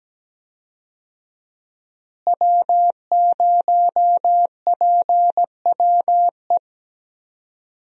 Morse code, heard as a keyed tone, about 17 wpm, 700 Hz. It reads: W0PWE
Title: USA Call Signs - 5 Characters - Repeated 3x - 17wpm